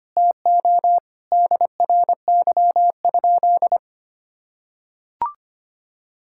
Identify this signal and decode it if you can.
Morse code, heard as a keyed tone, about 25 wpm, 700 Hz. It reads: TO DRY?
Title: Sentences from Top 1600 Words 25wpm